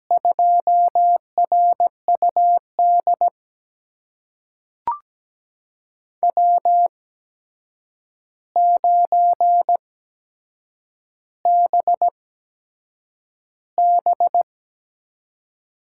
Morse code, heard as a keyed tone, about 17 wpm, spaced as for 7 wpm, 700 Hz. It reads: 2RUD W9BB
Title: USA Call Signs - 5 Characters - 8x Character Spacing - 17wpm